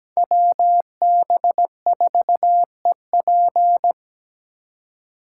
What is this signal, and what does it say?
Morse code, heard as a keyed tone, about 20 wpm, 700 Hz. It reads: WB4EP
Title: Speed Racing - USA 2x2 Call Signs 17wpm